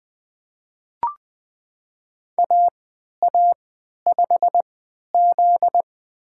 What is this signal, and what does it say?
Morse code, heard as a keyed tone, about 20 wpm, spaced as for 11 wpm, 700 Hz. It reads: AA5Z
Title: USA Call Signs - 4 Characters - 3x Character Spacing - 20wpm